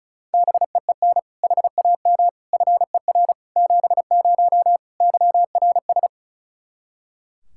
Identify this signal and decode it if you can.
Morse code, heard as a keyed tone, about 34 wpm, 700 Hz. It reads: BEEN HAM FER 70 YRS
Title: Practice QSO - Exchange 3 - More Info - 35wpm